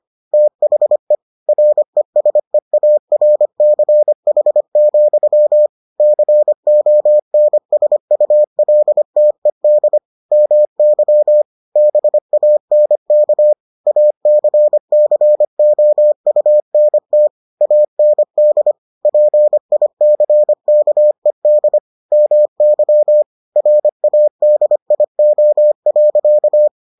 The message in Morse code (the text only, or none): THE RESEARCH, CONSULTED MY BANK ACCOUNT AND PICKED MY RADIO.